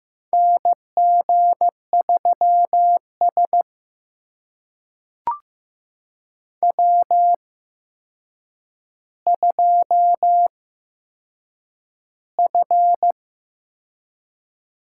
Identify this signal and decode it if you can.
Morse code, heard as a keyed tone, about 15 wpm, spaced as for 7 wpm, 700 Hz. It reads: NG3S W2F
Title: USA Call Signs - 4 Characters - 8x Character Spacing - 15wpm